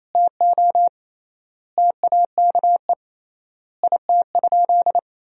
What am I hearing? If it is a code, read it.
Morse code, heard as a keyed tone, about 28 wpm, 700 Hz. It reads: TO TAKE IT?
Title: 3x Word-Spacing - Sentences from Top 300 Words - 28wpm